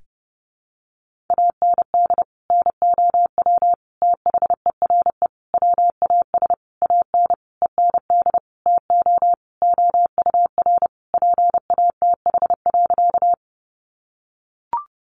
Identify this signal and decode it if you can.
Morse code, heard as a keyed tone, about 28 wpm, 700 Hz. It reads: AND NOW THERE WAS AN END TO OUR PATH.
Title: Sentences from Top 1400 Words 30wpm